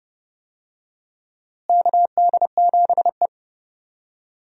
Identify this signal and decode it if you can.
Morse code, heard as a keyed tone, about 32 wpm, 700 Hz. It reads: KD7E
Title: USA Call Signs - 4 Characters - Repeated 2x - 30wpm